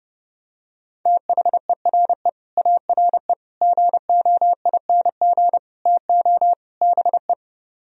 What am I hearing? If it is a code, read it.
Morse code, heard as a keyed tone, about 32 wpm, 700 Hz. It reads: THERE ARE GOING TO BE